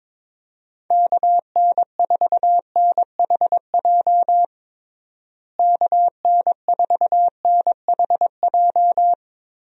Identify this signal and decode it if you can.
Morse code, heard as a keyed tone, about 21 wpm, 700 Hz. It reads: KN4NHJ KN4NHJ
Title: USA Call Signs - 6 Characters - Repeated 2x - 22wpm